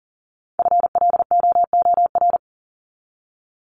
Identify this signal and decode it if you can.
Morse code, heard as a keyed tone, about 40 wpm, 700 Hz. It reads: FLOOR
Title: Top 1751-2000 Words 40wpm